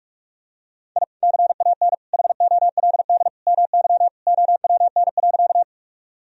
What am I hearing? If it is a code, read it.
Morse code, heard as a keyed tone, about 45 wpm, 700 Hz. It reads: I CAN HOLD MY OWN.
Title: Sentences from Top 1000 Words - Review Part 2 - 45wpm